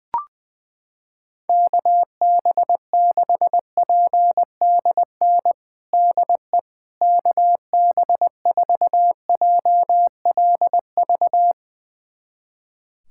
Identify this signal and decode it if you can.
Morse code, heard as a keyed tone, about 20 wpm, 700 Hz. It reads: KB6PDN DE KB4JLV
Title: USA Callsign DE Callsign - 6 Characters - 20wpm